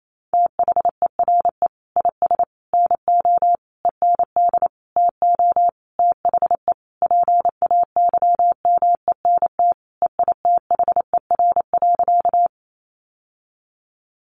Morse code, heard as a keyed tone, about 28 wpm, 700 Hz. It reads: THERE IS NO END TO THE PAYMENT EITHER.
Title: Sentences from Top 1200 Words 28wpm